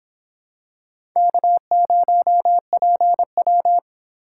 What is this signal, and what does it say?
Morse code, heard as a keyed tone, about 26 wpm, 700 Hz. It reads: K0PW